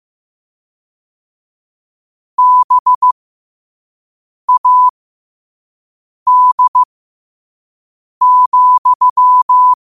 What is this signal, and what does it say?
Morse code, heard as a keyed tone, about 15 wpm, spaced as for 5 wpm, 1000 Hz. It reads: BAD,